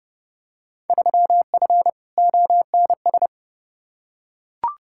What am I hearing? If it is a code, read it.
Morse code, heard as a keyed tone, about 32 wpm, 700 Hz. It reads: 3F ONS
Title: Field Day Exchanges - 30wpm